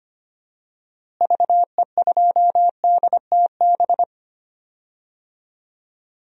Morse code, heard as a keyed tone, about 25 wpm, 700 Hz. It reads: VE2DTB
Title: DXCC-100 Call Signs - 6 Characters - Repeated 3x - 25wpm